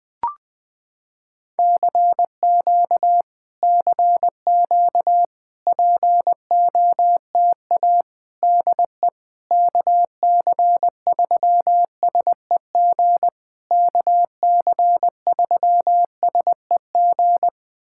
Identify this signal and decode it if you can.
Morse code, heard as a keyed tone, about 20 wpm, 700 Hz. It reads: CQ CQ POTA DE KC3SEG KC3SEG
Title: POTA QSO Practice - 20wpm